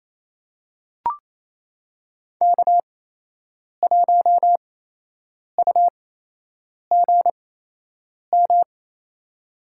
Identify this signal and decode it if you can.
Morse code, heard as a keyed tone, about 28 wpm, spaced as for 8 wpm, 700 Hz. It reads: K1UGM
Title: USA Call Signs - 5 Characters - 8x Character Spacing - 28wpm